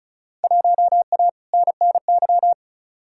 Morse code, heard as a keyed tone, about 34 wpm, 700 Hz. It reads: 1A NNY
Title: Field Day Exchanges - 35wpm